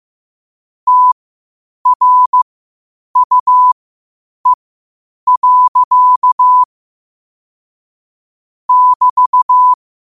Morse code, heard as a keyed tone, about 15 wpm, spaced as for 8 wpm, 1000 Hz. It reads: TRUE. <BT>
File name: qotdfiles / 8wpm